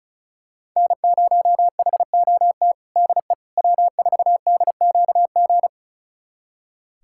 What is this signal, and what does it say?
Morse code, heard as a keyed tone, about 36 wpm, 700 Hz. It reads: N0HOT DE W4DQG
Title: USA Callsign DE Callsign - 5 Characters - 35wpm